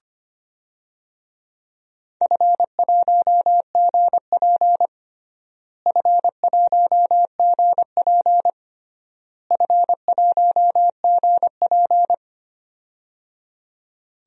Morse code, heard as a keyed tone, about 25 wpm, 700 Hz. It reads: F1GP F1GP F1GP
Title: DXCC-100 Call Signs - 4 Characters - Repeated 3x - 25wpm